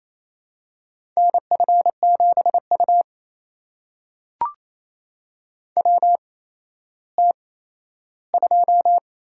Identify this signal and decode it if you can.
Morse code, heard as a keyed tone, about 28 wpm, spaced as for 14 wpm, 700 Hz. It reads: NF7U WT2
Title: USA Call Signs - 4 Characters - 8x Character Spacing - 28wpm